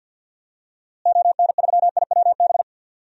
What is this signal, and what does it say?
Morse code, heard as a keyed tone, about 48 wpm, 700 Hz. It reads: ON3IWB
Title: DXCC-100 Call Signs - 6 Characters - Repeated 3x - 50wpm